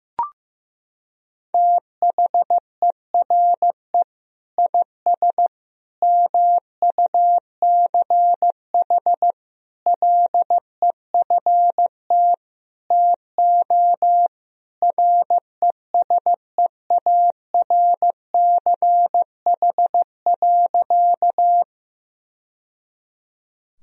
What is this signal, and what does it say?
Morse code, heard as a keyed tone, about 15 wpm, 700 Hz. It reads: THERE IS MUCH LEFT TO RESEARCH.
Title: Sentences from Top 1000 Words - Review Part 4 - 15wpm